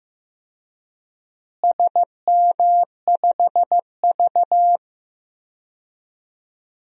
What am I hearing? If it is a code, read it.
Morse code, heard as a keyed tone, about 15 wpm, 700 Hz. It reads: SM5V
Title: DXCC-100 Call Signs - 4 Characters - Repeated 2x - 15wpm